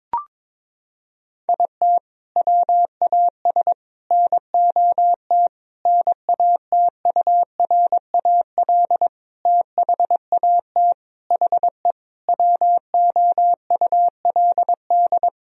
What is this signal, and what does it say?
Morse code, heard as a keyed tone, about 22 wpm, 700 Hz. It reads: IT WAS NOT NATURAL THAT HE WOULD
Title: Sentences from Top 1000 Words - Review Part 4 - 22wpm